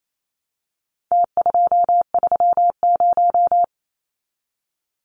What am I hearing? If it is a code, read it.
Morse code, heard as a keyed tone, about 28 wpm, 700 Hz. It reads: T230
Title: Numbers - Contest Style - 4-Digit 28wpm